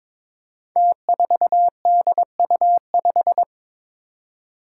Morse code, heard as a keyed tone, about 22 wpm, 700 Hz. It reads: T4DU5
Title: Five Challenging Letter-Number Combinations 22wpm